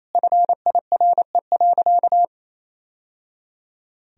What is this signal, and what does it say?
Morse code, heard as a keyed tone, about 28 wpm, 700 Hz. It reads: FIRE.
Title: Sentences from Top 1000 Words - Review Part 3 - 28wpm